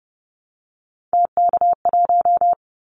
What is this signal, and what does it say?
Morse code, heard as a keyed tone, about 30 wpm, 700 Hz. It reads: TK1
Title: Callsign Prefixes for Top 100 DXCC - 30wpm